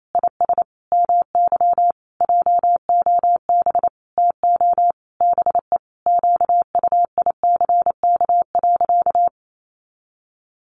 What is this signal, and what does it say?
Morse code, heard as a keyed tone, about 28 wpm, 700 Hz. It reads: IS MY JOB TO BE QUICK.
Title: Sentences from Top 600 Words 28wpm